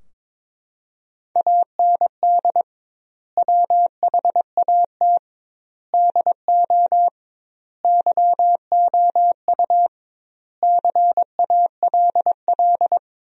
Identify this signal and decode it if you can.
Morse code, heard as a keyed tone, about 21 wpm, 700 Hz. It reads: AND WHAT DO YOU CALL
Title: 2x Word-Spacing - Sentences from Top 100 Words - 22wpm